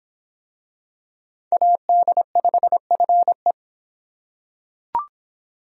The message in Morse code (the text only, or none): AD5FE